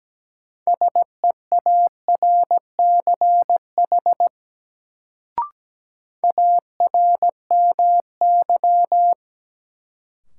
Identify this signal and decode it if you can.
Morse code, heard as a keyed tone, about 17 wpm, 700 Hz. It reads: SEARCH ARMY
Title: Sentences from Top 800 Words - New Word Drill - 17wpm